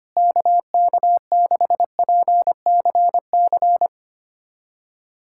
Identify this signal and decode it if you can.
Morse code, heard as a keyed tone, about 25 wpm, 700 Hz. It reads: KK6PCC
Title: USA Callsign DE Callsign 25wpm